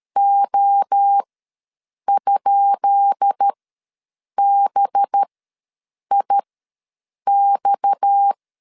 Morse code, heard as a keyed tone, about 13 wpm, spaced as for 7 wpm, 800 Hz. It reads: O?BIX
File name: LEZIONI / 10 SIGNS / 10D COMA ws 12wpm 3618